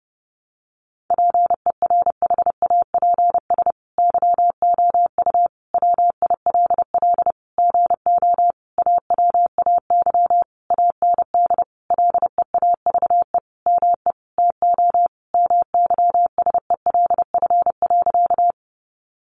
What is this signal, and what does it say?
Morse code, heard as a keyed tone, about 30 wpm, 700 Hz. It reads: PERHAPS YOU WILL GO AWAY AND LEAVE ME TO MYSELF.
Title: Sentences from Top 1300 Words 30wpm